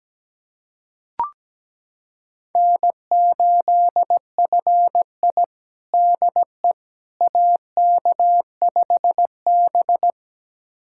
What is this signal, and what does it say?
Morse code, heard as a keyed tone, about 17 wpm, 700 Hz. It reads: N8FI DE AK5B
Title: USA Callsign DE Callsign - 4 Characters - 17wpm